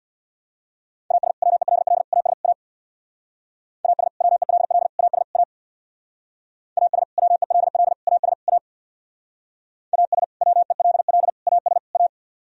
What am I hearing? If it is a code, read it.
Morse code, heard as a keyed tone, about 62 wpm, 700 Hz. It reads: AS WELL AS A AS WELL AS A AS WELL AS A AS WELL AS A
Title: Speed-Racing - Sets of 4 Words 50wpm